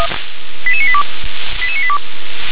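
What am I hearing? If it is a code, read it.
Five-tone selective call (ZVEI), heard as quick runs of five tones, tones of about 70 ms. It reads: 88982 88982
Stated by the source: ZVEI five-tone call received on SDR.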